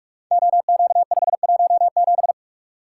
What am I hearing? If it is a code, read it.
Morse code, heard as a keyed tone, about 44 wpm, 700 Hz. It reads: OQ517
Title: Code Groups - 5 Letter:Numbers - 45wpm